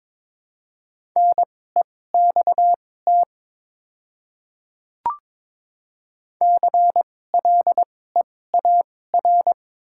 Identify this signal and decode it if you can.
Morse code, heard as a keyed tone, about 22 wpm, spaced as for 16 wpm, 700 Hz. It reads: NEXT CLEAR
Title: 2x Character Spacing - Top 500 Words - 22wpm